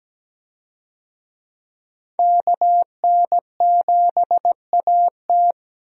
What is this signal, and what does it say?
Morse code, heard as a keyed tone, about 17 wpm, 700 Hz. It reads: KN7AT
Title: USA Call Signs - 5 Characters - 17wpm